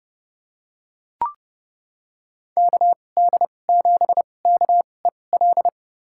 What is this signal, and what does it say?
Morse code, heard as a keyed tone, about 30 wpm, spaced as for 22 wpm, 700 Hz. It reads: KD7KEL
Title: USA Call Signs - 6 Characters - 2x Character Spacing - 30wpm